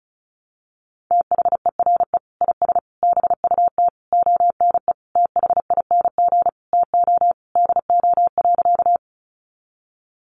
Morse code, heard as a keyed tone, about 34 wpm, 700 Hz. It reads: THERE IS BUT ONE THING TO DO.
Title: Sentences from Top 100 Words 35wpm